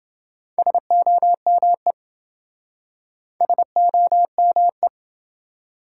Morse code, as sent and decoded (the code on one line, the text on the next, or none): ... --- -- . / ... --- -- .
SOME SOME